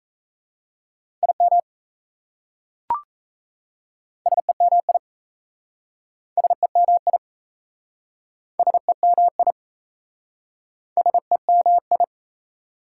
Morse code, heard as a keyed tone, about 37 wpm, 690 Hz. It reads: IM SEMI SEMI SEMI SEMI